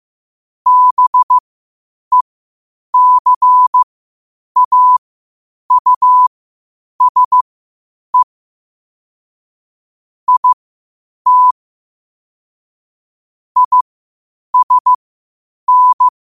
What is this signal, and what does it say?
Morse code, heard as a keyed tone, about 15 wpm, spaced as for 8 wpm, 1000 Hz. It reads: BECAUSE IT ISN
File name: qotdfiles / 8wpm